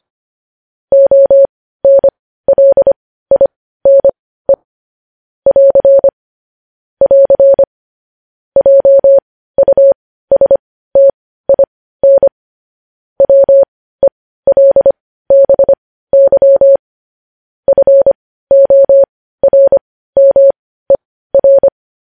Morse code, heard as a keyed tone, about 25 wpm, spaced as for 15 wpm, 550 Hz. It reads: ONLINE <AR> <AR> JUSTIN WELBY FORMER
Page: news-headlines-in-morse-code-at-15-wpm